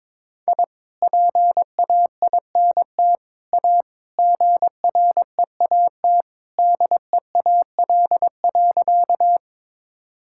Morse code, heard as a keyed tone, about 22 wpm, 700 Hz. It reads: I PAINT A GREAT DEAL.
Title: Sentences from Top 1000 Words - Review Part 6 - 22wpm